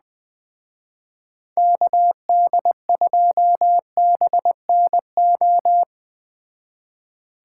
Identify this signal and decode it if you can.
Morse code, heard as a keyed tone, about 20 wpm, 700 Hz. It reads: KD2BNO
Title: Speed Racing - USA 2x3 Call Signs 17wpm